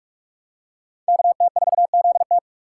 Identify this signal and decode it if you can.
Morse code, heard as a keyed tone, about 46 wpm, 700 Hz. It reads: KT4ZT
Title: USA Call Signs - 5 Characters - 6x Character Spacing - 45wpm